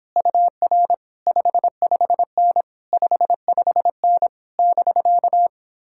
Morse code, heard as a keyed tone, about 29 wpm, 700 Hz. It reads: UR 55N 55N <BK>